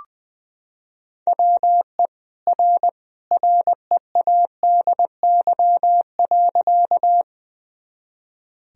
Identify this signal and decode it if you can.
Morse code, heard as a keyed tone, about 20 wpm, 700 Hz. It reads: WE R READY.
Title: Sentences from Top 400 Words and Abbreviations 20wpm